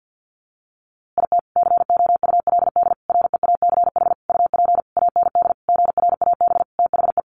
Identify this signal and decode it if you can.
Morse code, heard as a keyed tone, about 50 wpm, 700 Hz. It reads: IT COULD REACH UP AND GRAB THE
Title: A Sound Of Thunder 50wpm